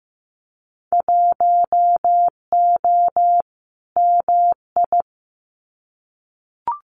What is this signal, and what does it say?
Morse code, heard as a keyed tone, about 15 wpm, 700 Hz. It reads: 1O MI
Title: Winter Field Day Exchanges 15wpm